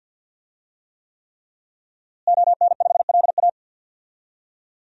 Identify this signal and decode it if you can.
Morse code, heard as a keyed tone, about 52 wpm, 700 Hz. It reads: ON5LA
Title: DXCC-100 Call Signs - 5 Characters - Repeated 3x - 50wpm